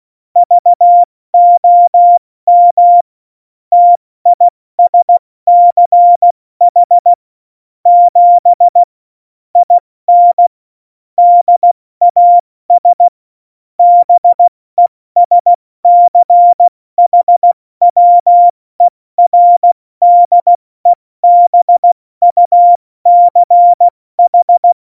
Morse code, heard as a keyed tone, about 16 wpm, 700 Hz. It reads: VOM TISCH 7 IN DAS BESCHWERDEBUCH
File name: Morsekurs / Witze / Witz 10. September 2019, 70 F80